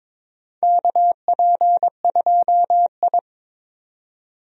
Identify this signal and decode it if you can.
Morse code, heard as a keyed tone, about 22 wpm, 700 Hz. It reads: KP2I
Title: DXCC-100 Call Signs - 4 Characters - 22wpm